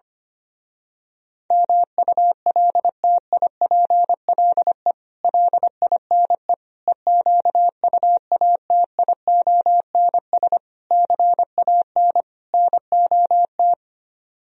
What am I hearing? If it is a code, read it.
Morse code, heard as a keyed tone, about 24 wpm, 700 Hz. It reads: MULTIPLE LINE EQUATIONS CAN NOT
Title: Speed-Racing - Sets of 5 Words 25wpm